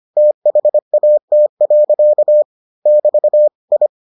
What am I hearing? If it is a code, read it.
Morse code, heard as a keyed tone, about 25 wpm, 600 Hz. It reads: THAT. <BT> I